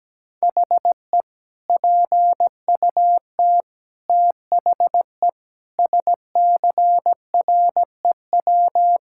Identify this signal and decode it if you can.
Morse code, heard as a keyed tone, about 17 wpm, 700 Hz. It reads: HE PUT THE SCREW
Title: Sentences from Top 1600 Words 17wpm